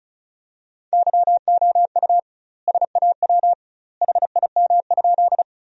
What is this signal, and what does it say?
Morse code, heard as a keyed tone, about 34 wpm, 700 Hz. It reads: YOU SAW HIM?
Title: 2x Word-Spacing - Sentences from Top 300 Words - 35wpm